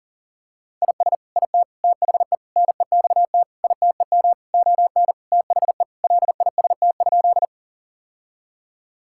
Morse code, heard as a keyed tone, about 40 wpm, 700 Hz. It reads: IS IT THE NEXT ITEM ON THE LIST?
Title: Sentences from Top 1000 Words - Review Part 6 - 40wpm